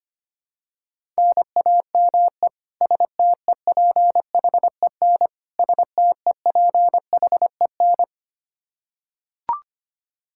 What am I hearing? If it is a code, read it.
Morse code, heard as a keyed tone, about 25 wpm, 700 Hz. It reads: NAME STEPHEN STEPHEN
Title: Practice QSO - Exchange 1 - Beginning - 25wpm